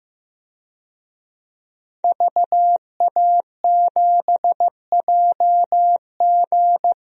Morse code, heard as a keyed tone, about 16 wpm, 700 Hz. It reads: VA7JG
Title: DXCC-100 Call Signs - 5 Characters - 15wpm